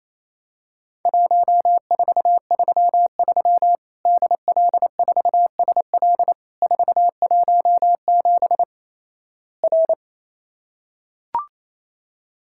Morse code, heard as a keyed tone, about 28 wpm, 700 Hz. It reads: 1433 DL4SL 417 R
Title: Worked All Europe DX Contest - QTC Exchange - 28wpm